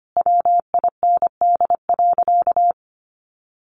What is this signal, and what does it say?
Morse code, heard as a keyed tone, about 25 wpm, 700 Hz. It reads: WIND.
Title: Sentences from Top 1400 Words 25wpm